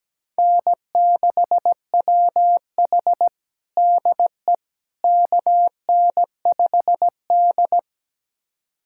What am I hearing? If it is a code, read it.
Morse code, heard as a keyed tone, about 17 wpm, 700 Hz. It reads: N6WH DE KN5D
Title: USA Callsign DE Callsign - 4 Characters - 17wpm